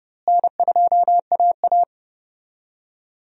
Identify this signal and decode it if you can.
Morse code, heard as a keyed tone, about 30 wpm, 700 Hz. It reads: N2AA
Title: USA Call Signs - 4 Characters - Repeated 3x - 30wpm